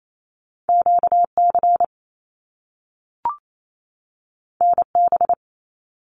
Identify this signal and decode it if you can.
Morse code, heard as a keyed tone, about 28 wpm, 700 Hz. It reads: QC NB
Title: Canadian Provinces and Territories 28wpm